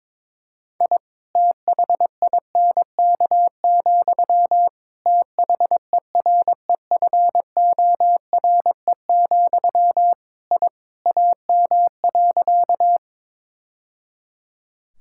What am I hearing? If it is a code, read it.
Morse code, heard as a keyed tone, about 22 wpm, 700 Hz. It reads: I THINK, THEREFORE, I AM.
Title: Sentences from Top 1000 Words - Review Part 5 - 22wpm